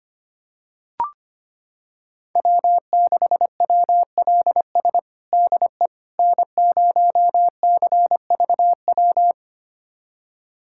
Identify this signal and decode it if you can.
Morse code, heard as a keyed tone, about 25 wpm, 700 Hz. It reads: W6WLS DE N0CVW
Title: USA Callsign DE Callsign - 5 Characters - 25wpm